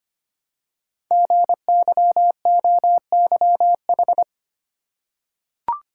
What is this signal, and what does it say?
Morse code, heard as a keyed tone, about 25 wpm, 700 Hz. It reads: GYOYH
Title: Code Groups - 5 Letter:Numbers - 25wpm